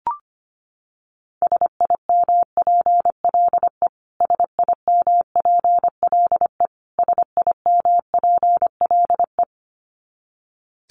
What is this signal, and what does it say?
Morse code, heard as a keyed tone, about 25 wpm, 700 Hz. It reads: SIMPLE SIMPLE SIMPLE
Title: Top 500 Words Repeated 25wpm